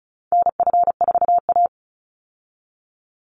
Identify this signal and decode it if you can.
Morse code, heard as a keyed tone, about 34 wpm, 700 Hz. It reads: NF4A
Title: Call Signs - 35wpm